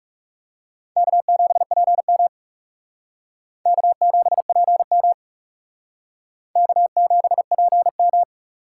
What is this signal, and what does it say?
Morse code, heard as a keyed tone, about 35 wpm, 700 Hz. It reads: K7PM K7PM K7PM
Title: Speed Racing - USA 1x2 Call Signs 30wpm